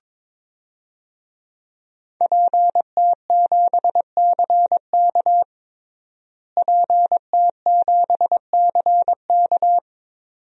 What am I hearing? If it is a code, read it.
Morse code, heard as a keyed tone, about 22 wpm, 700 Hz. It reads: PT7CK PT7CK
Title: DXCC-100 Call Signs - 5 Characters - Repeated 2x - 22wpm